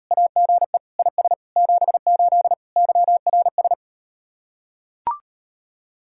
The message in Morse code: AGE IS 78 YRS

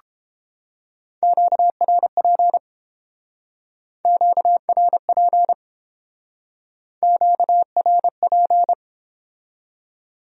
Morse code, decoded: QRP QRP QRP

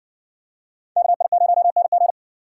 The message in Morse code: KI0NG

